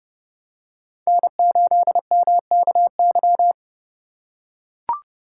Morse code, heard as a keyed tone, about 28 wpm, 700 Hz. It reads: N8MKY